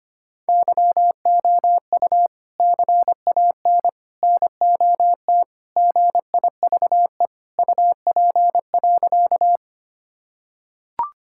Morse code, heard as a keyed tone, about 25 wpm, 700 Hz. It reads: YOU CAN NOT GIVE UP.
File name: practice / 25wpm / Sentences with 3-Word Phrases - 25wpm